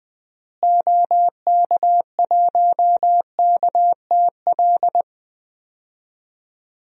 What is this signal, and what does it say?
Morse code, heard as a keyed tone, about 21 wpm, 700 Hz. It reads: OK1KTL
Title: DXCC-100 Call Signs - 6 Characters - Repeated 2x - 20wpm